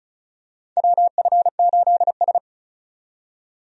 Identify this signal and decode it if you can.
Morse code, heard as a keyed tone, about 36 wpm, 700 Hz. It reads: WF8S
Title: USA Call Signs - 4 Characters - 35wpm